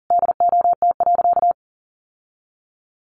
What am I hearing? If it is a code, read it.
Morse code, heard as a keyed tone, about 40 wpm, 700 Hz. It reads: DOT.
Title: Sentences from Top 1500 Words 40wpm